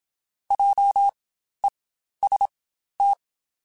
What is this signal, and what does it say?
Morse code, heard as a keyed tone, about 27 wpm, spaced as for 12 wpm, 800 Hz. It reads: JEST
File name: Spaced Alphas / Spaced Alphas mp3s / Js sp @25